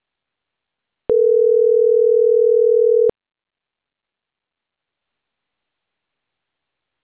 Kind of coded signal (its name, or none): telephone call progress tone